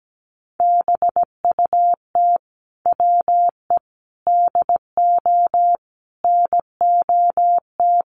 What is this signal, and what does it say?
Morse code, heard as a keyed tone, about 17 wpm, 700 Hz. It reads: BUT WE DO NOT
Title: Sets of 5 Words Encore2 17wpm